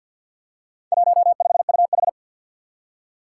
Morse code, heard as a keyed tone, about 51 wpm, 700 Hz. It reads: 15UH
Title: Four Challenging Letter-Number Combinations 50wpm